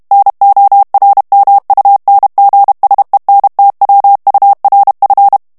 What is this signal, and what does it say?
Morse code, heard as a keyed tone, about 32 wpm, 800 Hz. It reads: NORMUNGSENTWURF
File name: ALLEMAND texte clair 1 vitesse 30 mots minute